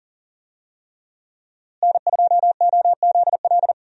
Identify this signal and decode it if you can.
Morse code, heard as a keyed tone, about 40 wpm, 700 Hz. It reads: N2OZL
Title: USA Call Signs - 5 Characters - Repeated 3x - 40wpm